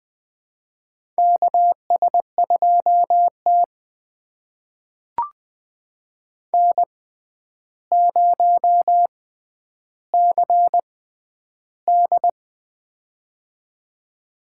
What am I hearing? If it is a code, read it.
Morse code, heard as a keyed tone, about 20 wpm, spaced as for 10 wpm, 700 Hz. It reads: KS2T N0CD
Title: USA Call Signs - 4 Characters - 6x Character Spacing - 20wpm